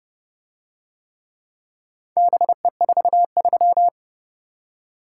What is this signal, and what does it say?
Morse code, heard as a keyed tone, about 31 wpm, 700 Hz. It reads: BE43